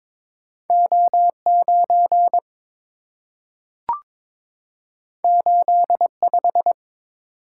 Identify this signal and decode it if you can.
Morse code, heard as a keyed tone, about 22 wpm, 690 Hz. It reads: O9 85
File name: practice / 22wpm / Two Letter-Number Combinations - 22wpm